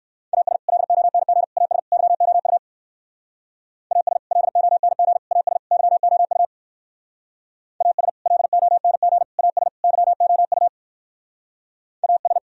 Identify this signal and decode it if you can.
Morse code, heard as a keyed tone, about 63 wpm, 700 Hz. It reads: AS LONG AS YOU AS LONG AS YOU AS LONG AS YOU AS